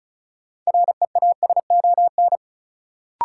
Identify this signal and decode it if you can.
Morse code, heard as a keyed tone, about 36 wpm, 700 Hz. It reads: REASON
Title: Sets of 5 Words Encore - New Word Drill - 35wpm